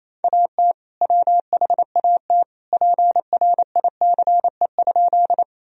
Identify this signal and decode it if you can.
Morse code, heard as a keyed tone, about 28 wpm, 700 Hz. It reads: AT WHAT PRICE?
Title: Sentences from Top 1000 Words - Review Part 4 - 28wpm